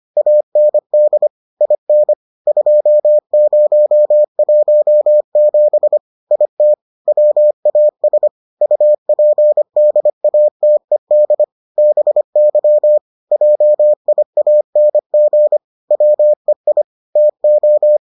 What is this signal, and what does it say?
Morse code, heard as a keyed tone, about 25 wpm, 600 Hz. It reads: AND IN 2017 IT WAS UPDATED BY JIANG WEI TO